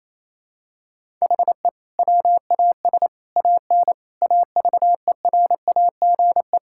Morse code, heard as a keyed tone, about 28 wpm, 700 Hz. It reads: HE WAS AN AVERAGE
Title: Sentences from Top 1000 Words - Review Part 6 - 28wpm